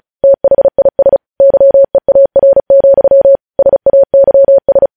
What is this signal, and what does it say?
Morse code, heard as a keyed tone, about 36 wpm, 550 Hz. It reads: THIS YEAR, SAYS
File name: podcast / News headlines-1x-35x35